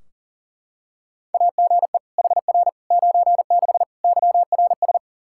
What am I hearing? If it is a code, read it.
Morse code, heard as a keyed tone, about 40 wpm, 700 Hz. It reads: AGE HR 96 YRS
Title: Practice QSO - Exchange 3 - More Info - 40wpm